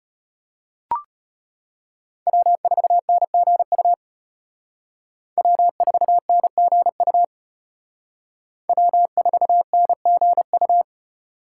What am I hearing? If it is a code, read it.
Morse code, heard as a keyed tone, about 32 wpm, 700 Hz. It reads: W4NGU W4NGU W4NGU